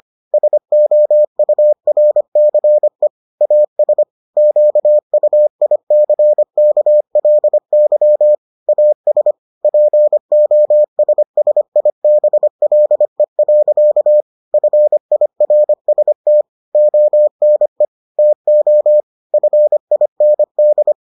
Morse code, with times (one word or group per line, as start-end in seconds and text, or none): 0.3-3.1 SOURCE
3.4-4.0 AS
4.4-8.3 QUICKLY
8.7-9.3 AS
9.6-14.2 POSSIBLE.
14.5-16.4 FIRST
16.8-17.9 ONE
18.2-19.0 TO
19.3-20.9 FIND